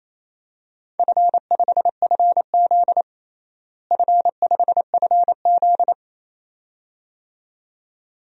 Morse code, heard as a keyed tone, about 28 wpm, 700 Hz. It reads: F5FZ F5FZ
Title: DXCC-100 Call Signs - 4 Characters - Repeated 2x - 28wpm